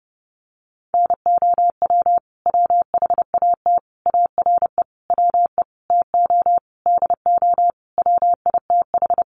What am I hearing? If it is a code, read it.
Morse code, heard as a keyed tone, about 30 wpm, 700 Hz. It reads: NOW WHAT ARE WE TO DO WITH
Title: Sentences from Top 100 Words 30wpm